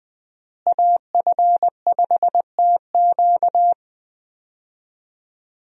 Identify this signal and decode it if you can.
Morse code, heard as a keyed tone, about 20 wpm, 700 Hz. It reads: AF5TQ
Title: USA Call Signs - 5 Characters - Repeated 2x - 20wpm